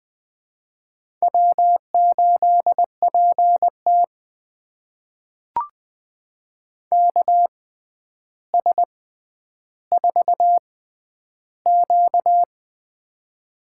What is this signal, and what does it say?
Morse code, heard as a keyed tone, about 20 wpm, spaced as for 10 wpm, 700 Hz. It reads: W8PT KS4Q
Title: USA Call Signs - 4 Characters - 6x Character Spacing - 20wpm